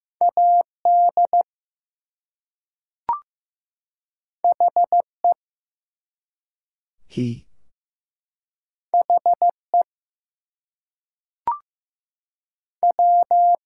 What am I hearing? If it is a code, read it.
Morse code, heard as a keyed tone, about 15 wpm, 700 Hz. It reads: AD HE HE W
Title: Two Letter Words - 15wpm